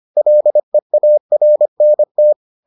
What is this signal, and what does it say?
Morse code, heard as a keyed tone, about 25 wpm, 600 Hz. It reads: LEARNT